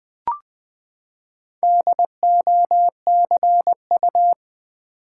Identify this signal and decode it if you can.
Morse code, heard as a keyed tone, about 20 wpm, 700 Hz. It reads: DOCU
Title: Build A Word - 4 Letters At A Time - 11-Letter Words - 20wpm